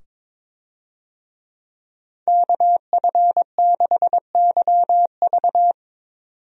Morse code, heard as a keyed tone, about 22 wpm, 700 Hz. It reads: KF6YV
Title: USA Call Signs - 5 Characters - 22wpm